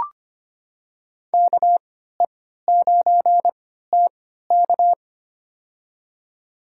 Morse code, heard as a keyed tone, about 25 wpm, spaced as for 14 wpm, 700 Hz. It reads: KE9TK
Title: USA Call Signs - 5 Characters - 3x Character Spacing - 25wpm